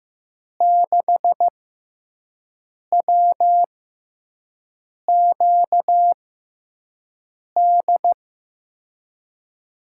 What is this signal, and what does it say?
Morse code, heard as a keyed tone, about 15 wpm, spaced as for 5 wpm, 700 Hz. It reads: 6WQD
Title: USA Call Signs - 6 Characters - 6x Character Spacing - 15wpm